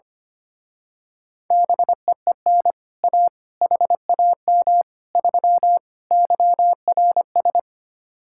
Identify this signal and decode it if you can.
Morse code, heard as a keyed tone, about 25 wpm, 700 Hz. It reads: BEEN A HAM 3 YRS
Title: Speed-Racing Common QSO Phrases - 25wpm